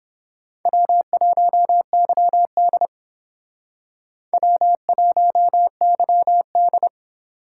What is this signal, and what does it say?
Morse code, heard as a keyed tone, about 29 wpm, 700 Hz. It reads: W1YD W1YD